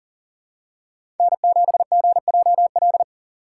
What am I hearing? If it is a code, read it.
Morse code, heard as a keyed tone, about 40 wpm, 700 Hz. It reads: N7GJL
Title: USA Call Signs - 5 Characters - 6x Character Spacing - 40wpm